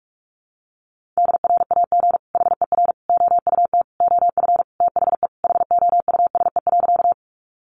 Morse code, heard as a keyed tone, about 44 wpm, 700 Hz. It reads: DRAG HER OUT OF THE HOUSE.
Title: Sentences from Top 1500 Words 45wpm